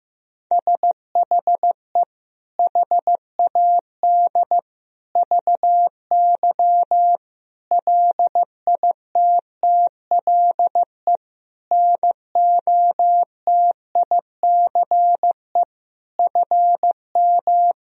Morse code, heard as a keyed tone, about 15 wpm, 700 Hz. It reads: SHE HAD VY LITTLE NOTICE FM
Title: Sentences from Top 400 Words and Abbreviations 15wpm